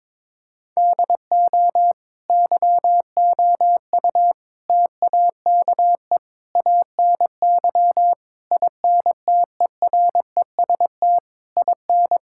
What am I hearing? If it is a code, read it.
Morse code, heard as a keyed tone, about 22 wpm, 700 Hz. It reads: DO YOU TAKE ANY INTEREST IN
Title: Sentences from Top 1000 Words - Review Part 6 - 22wpm